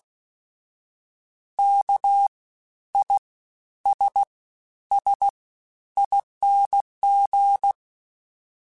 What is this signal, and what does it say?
Morse code, heard as a keyed tone, about 16 wpm, 800 Hz. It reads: K I S S ING
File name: Special INGs-mp3s / Special INGs-mp3s / SlowToFaster ing @15